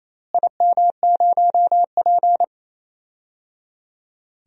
Morse code, heard as a keyed tone, about 29 wpm, 700 Hz. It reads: IM0P
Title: DXCC-100 Call Signs - 4 Characters - Repeated 3x - 28wpm